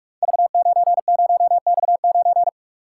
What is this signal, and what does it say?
Morse code, heard as a keyed tone, about 44 wpm, 700 Hz. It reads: V90X9
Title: Code Groups - 5 Letter:Numbers - 45wpm